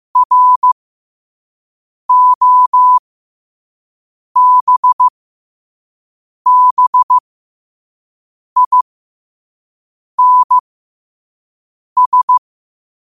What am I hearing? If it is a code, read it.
Morse code, heard as a keyed tone, about 15 wpm, spaced as for 5 wpm, 1000 Hz. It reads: ROBBINS
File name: qotdfiles / 5wpm